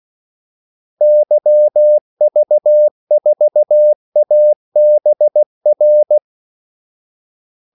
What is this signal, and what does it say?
Morse code, heard as a keyed tone, about 16 wpm, 600 Hz. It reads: YV4ABR
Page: Kaldesignaler 16 wpm